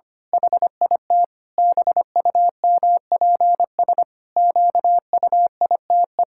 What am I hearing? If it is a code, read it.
Morse code, heard as a keyed tone, about 25 wpm, 700 Hz. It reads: HIT BUMPS QUITE
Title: Pangram Challenge v2 - 25wpm